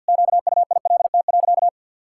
Morse code, heard as a keyed tone, about 50 wpm, 700 Hz. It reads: QUILT.
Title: Pangram Challenge v2 - 50wpm